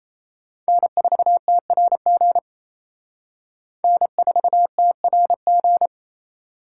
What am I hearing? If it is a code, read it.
Morse code, heard as a keyed tone, about 35 wpm, 700 Hz. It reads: N4TRG N4TRG
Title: Speed Racing - USA 1x3 Call Signs 28wpm